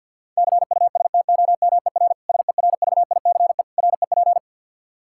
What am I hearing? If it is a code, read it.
Morse code, heard as a keyed tone, about 53 wpm, 700 Hz. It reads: CUSTOMER SERVICE REP
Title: Practice QSO - Exchange 3 - More Info - 50wpm